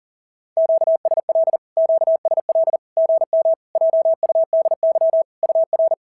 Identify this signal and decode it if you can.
Morse code, heard as a keyed tone, about 40 wpm, 650 Hz. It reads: QSL QSL GM JUDY UR
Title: SOTA QSO Practice 40wpm